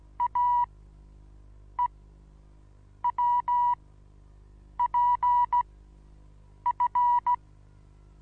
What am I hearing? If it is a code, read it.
Morse code, heard as a keyed tone, about 16 wpm, spaced as for 6 wpm, 970 Hz. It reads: AEWPF
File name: Onlinelehrgang / morsen / DJ4UF Morsen25